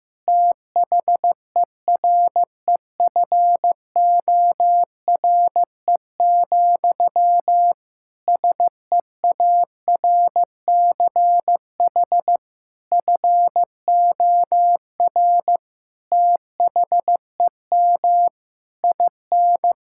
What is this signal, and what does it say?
Morse code, heard as a keyed tone, about 15 wpm, 700 Hz. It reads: THEREFORE, SEARCH FOR THEM IN
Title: Sentences from Top 1000 Words - Review Part 5 - 15wpm